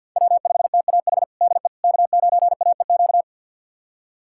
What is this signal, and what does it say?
Morse code, heard as a keyed tone, about 51 wpm, 700 Hz. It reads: W5TAH DE K9AEQ
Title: USA Callsign DE Callsign - 5 Characters - 50wpm